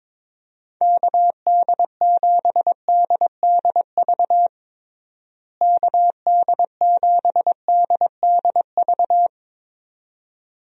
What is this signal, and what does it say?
Morse code, heard as a keyed tone, about 21 wpm, 700 Hz. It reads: KD7DDV KD7DDV
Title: USA Call Signs - 6 Characters - Repeated 2x - 22wpm